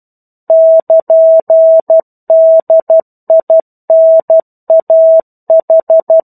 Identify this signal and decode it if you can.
Morse code, heard as a keyed tone, about 12 wpm, 650 Hz. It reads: <KN>DINAH
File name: 01-Chapter-01 12 WPM